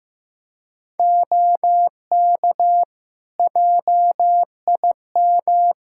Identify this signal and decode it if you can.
Morse code, heard as a keyed tone, about 15 wpm, 700 Hz. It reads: OK JIM
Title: Practice QSO - Exchange 3 - More Info - 15wpm